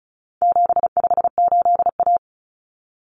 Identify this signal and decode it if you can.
Morse code, heard as a keyed tone, about 34 wpm, 700 Hz. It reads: 758A
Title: Common Cut Numbers - 4-Digit 35wpm